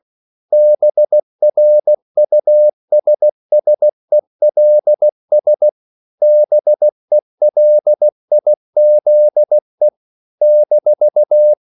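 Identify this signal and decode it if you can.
Morse code, heard as a keyed tone, about 16 wpm, 600 Hz. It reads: BRUSSELS BELIZE -
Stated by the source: CW practice of national capitals at 16 words per minute, from ON6ZQ.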